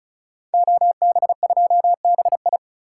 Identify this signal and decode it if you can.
Morse code, heard as a keyed tone, about 36 wpm, 700 Hz. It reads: OB2BI
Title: Code Groups - 5 Letter:Numbers - 35wpm